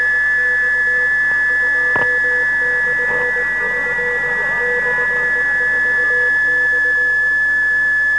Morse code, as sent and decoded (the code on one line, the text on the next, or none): none